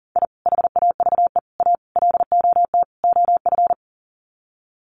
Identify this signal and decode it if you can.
Morse code, heard as a keyed tone, about 40 wpm, 700 Hz. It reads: I HAVE A LOT OF
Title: Sets of 5 Words 40wpm